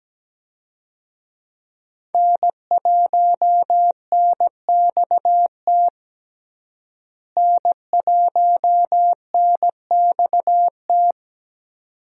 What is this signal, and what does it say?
Morse code, heard as a keyed tone, about 17 wpm, 700 Hz. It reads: N1NXT N1NXT
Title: USA Call Signs - 5 Characters - Repeated 2x - 17wpm